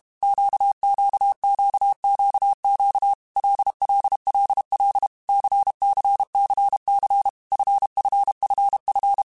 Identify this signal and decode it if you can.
Morse code, heard as a keyed tone, about 32 wpm, 800 Hz. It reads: QQQQQ LLLL CCCC FFFF
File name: Groups@30 / Groups@30 mp3s / LFCYQ@30